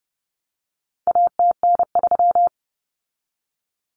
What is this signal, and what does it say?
Morse code, heard as a keyed tone, about 30 wpm, 700 Hz. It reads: ATN3
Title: Common Cut Numbers - 4-Digit 30wpm